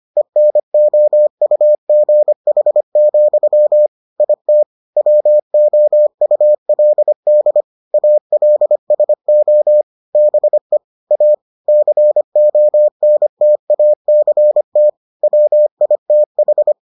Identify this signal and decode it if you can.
Morse code, heard as a keyed tone, about 25 wpm, 600 Hz. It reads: ENOUGH, IT WOULD ALSO BE A CONTACT WITH